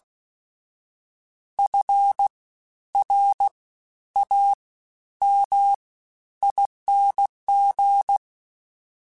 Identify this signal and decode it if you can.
Morse code, heard as a keyed tone, about 16 wpm, spaced as for 11 wpm, 800 Hz. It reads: FRAMING
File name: Special INGs-mp3s / Special INGs-mp3s / SlowToFaster ing @15